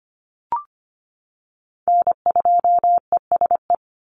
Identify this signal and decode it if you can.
Morse code, heard as a keyed tone, about 25 wpm, 700 Hz. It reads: N2ESE
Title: NA SOTA Chaser Callsigns 25wpm